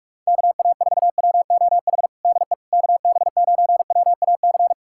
Morse code, heard as a keyed tone, about 47 wpm, 700 Hz. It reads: KA4WOH DE KB9WAC
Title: USA Callsign DE Callsign 45wpm